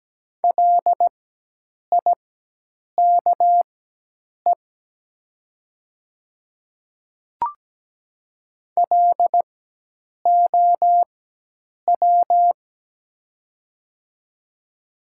Morse code, heard as a keyed tone, about 17 wpm, spaced as for 8 wpm, 690 Hz. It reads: LIKE LOW
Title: 4x Character Spacing - Top 200 Words - 17wpm